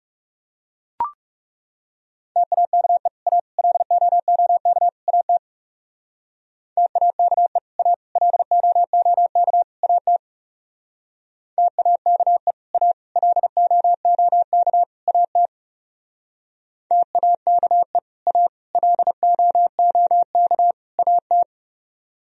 Morse code, heard as a keyed tone, about 40 wpm, 700 Hz. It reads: TAKE A LOOK AT TAKE A LOOK AT TAKE A LOOK AT TAKE A LOOK AT